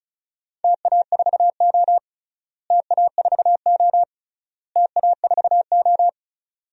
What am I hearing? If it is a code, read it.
Morse code, heard as a keyed tone, about 36 wpm, 700 Hz. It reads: TA4O TA4O TA4O
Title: DXCC-100 Call Signs - 4 Characters - Repeated 3x - 35wpm